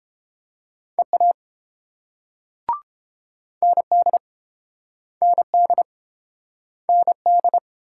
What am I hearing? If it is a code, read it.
Morse code, heard as a keyed tone, about 29 wpm, 700 Hz. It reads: EA ND ND ND